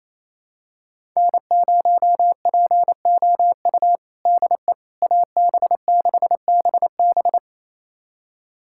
Morse code, heard as a keyed tone, about 28 wpm, 700 Hz. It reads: N0POU DE AB6BB
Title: USA Callsign DE Callsign - 5 Characters - 28wpm